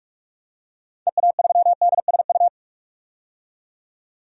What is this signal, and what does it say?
Morse code, heard as a keyed tone, about 46 wpm, 700 Hz. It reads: EA3DSU
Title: DXCC-100 Call Signs - 6 Characters - 45wpm